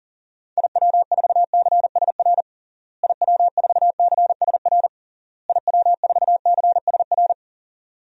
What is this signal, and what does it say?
Morse code, heard as a keyed tone, about 40 wpm, 700 Hz. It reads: IW4CSR IW4CSR IW4CSR
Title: DXCC-100 Call Signs - 6 Characters - Repeated 3x - 40wpm